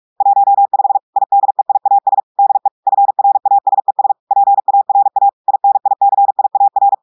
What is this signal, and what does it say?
Morse code, heard as a keyed tone, about 46 wpm, 800 Hz. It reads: 15 IDEIAS DE FRASES PARA INICIAR
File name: audios / portugais / portugais texte en clair 2 vitesse 45 mots minutes